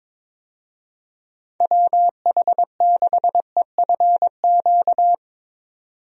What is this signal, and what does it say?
Morse code, heard as a keyed tone, about 22 wpm, 700 Hz. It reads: WH6EFQ
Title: USA Call Signs - 6 Characters - 22wpm